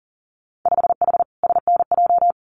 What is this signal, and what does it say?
Morse code, heard as a keyed tone, about 40 wpm, 700 Hz. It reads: 5H SNJ
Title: Winter Field Day Exchanges Encore 40wpm